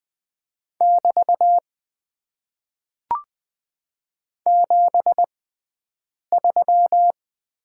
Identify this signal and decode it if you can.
Morse code, heard as a keyed tone, about 20 wpm, spaced as for 7 wpm, 700 Hz. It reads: <BT> 73
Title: Top 90 QSO Elements - 6x Character Spacing - 20wpm